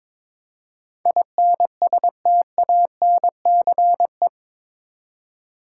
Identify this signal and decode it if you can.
Morse code, heard as a keyed tone, about 22 wpm, 700 Hz. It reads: INSTANCE
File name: practice / 22wpm / Eight Letter Words - 22wpm